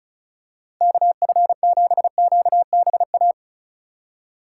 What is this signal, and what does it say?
Morse code, heard as a keyed tone, about 36 wpm, 700 Hz. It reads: KF7QBA
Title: USA Call Signs - 2x Character Spacing - 35wpm